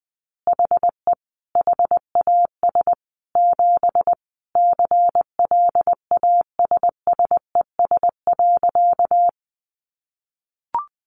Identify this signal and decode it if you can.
Morse code, heard as a keyed tone, about 19 wpm, 700 Hz. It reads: HE HAS 7 CLASSES.
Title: Additional Easy Phrases 20wpm